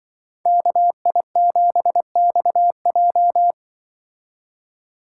Morse code, heard as a keyed tone, about 24 wpm, 700 Hz. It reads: KI7XJ
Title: Speed Racing - USA 2x2 Call Signs 20wpm